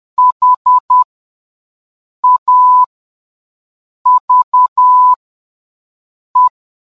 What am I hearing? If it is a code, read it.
Morse code, heard as a keyed tone, about 10 wpm, spaced as for 5 wpm, 1000 Hz. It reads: HAVE